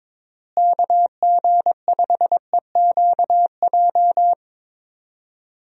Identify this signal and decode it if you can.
Morse code, heard as a keyed tone, about 22 wpm, 700 Hz. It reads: KG5EQJ
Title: USA Call Signs - 6 Characters - Repeated 3x - 22wpm